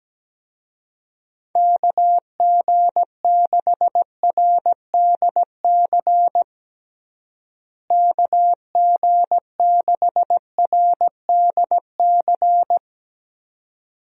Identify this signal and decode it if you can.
Morse code, heard as a keyed tone, about 17 wpm, 700 Hz. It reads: KG6RDC KG6RDC
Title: USA Call Signs - 6 Characters - Repeated 2x - 17wpm